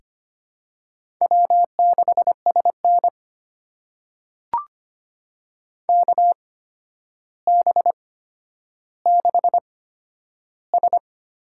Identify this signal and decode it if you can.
Morse code, heard as a keyed tone, about 25 wpm, spaced as for 11 wpm, 700 Hz. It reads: W6SN KB6S
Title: USA Call Signs - 8x Character Spacing - 25wpm